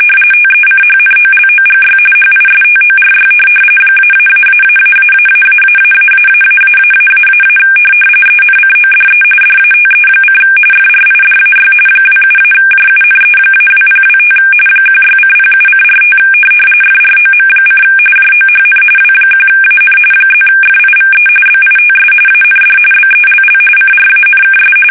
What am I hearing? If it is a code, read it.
RTTY, about 45.45 baud. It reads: TECHNOLOGY HISTORY ---------------------- ENCODE DATE: FRIDAY 27-JUNE-2025 16:07:45 FROM: JOHN W1AN +FIVE BITS AND BELL CRANKS
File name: fsk850 / benjamin-g-lamme